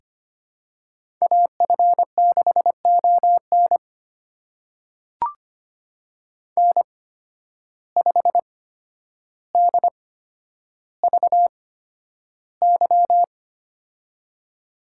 Morse code, heard as a keyed tone, about 25 wpm, spaced as for 11 wpm, 700 Hz. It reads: AF6ON N5DVY
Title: USA Call Signs - 8x Character Spacing - 25wpm